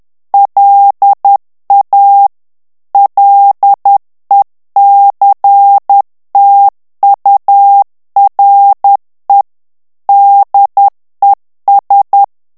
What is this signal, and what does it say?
Morse code, heard as a keyed tone, about 10 wpm, 800 Hz. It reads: LA LECTURE DES
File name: morse mp3 / FRANCAIS TEXTE CLAIR 2 VITESSE 10 MOTS MINUTES